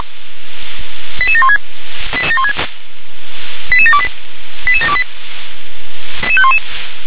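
Five-tone selective call (ZVEI), heard as quick runs of five tones, tones of about 70 ms. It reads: 88716 88728 88728
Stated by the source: Five-tone ZVEI selective call received on an SDR.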